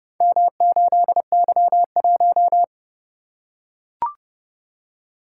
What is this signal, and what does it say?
Morse code, heard as a keyed tone, about 32 wpm, 700 Hz. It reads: M8Y1